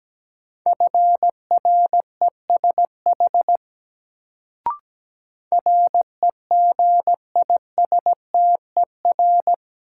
Morse code, heard as a keyed tone, about 17 wpm, 700 Hz. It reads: FRESH REGISTER
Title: Sentences from Top 1200 Words - New Word Drill Alt - 17wpm